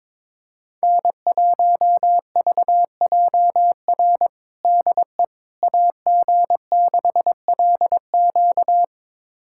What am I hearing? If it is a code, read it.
Morse code, heard as a keyed tone, about 22 wpm, 700 Hz. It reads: N1VJR DE AG6LQ
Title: USA Callsign DE Callsign - 5 Characters - 22wpm